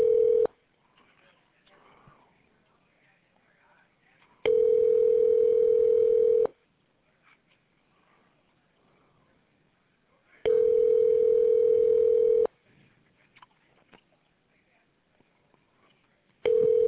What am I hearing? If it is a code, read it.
Ringback tone (North America), 440 + 480 Hz.